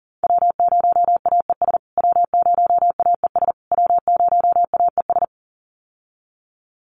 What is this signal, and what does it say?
Morse code, heard as a keyed tone, about 40 wpm, 700 Hz. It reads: W0AES W0AES W0AES
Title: USA Call Signs Repeated 3x 40wpm